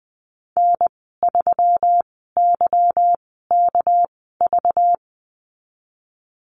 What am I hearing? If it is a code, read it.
Morse code, heard as a keyed tone, about 19 wpm, spaced as for 14 wpm, 700 Hz. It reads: N3YKV
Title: 200 Lesson 32g - Letter k - Call Signs 20wpm